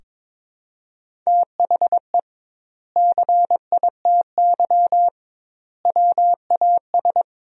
Morse code, heard as a keyed tone, about 22 wpm, 700 Hz. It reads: THE CITY WAS
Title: 2x Word-Spacing - Sentences from Top 500 Words - 22wpm